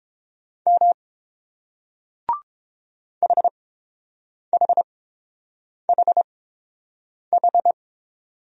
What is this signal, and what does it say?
Morse code, heard as a keyed tone, about 31 wpm, spaced as for 7 wpm, 700 Hz. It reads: M HHHH